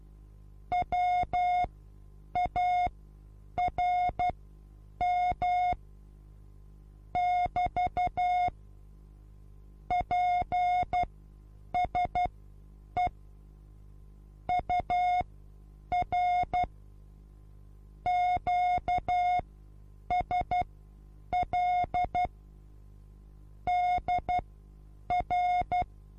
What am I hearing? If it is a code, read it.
Morse code, heard as a keyed tone, about 12 wpm, spaced as for 8 wpm, 720 Hz. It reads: WARM <BT> PSE UR QSL DR
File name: Onlinelehrgang / morsen / DJ4UF Morsen23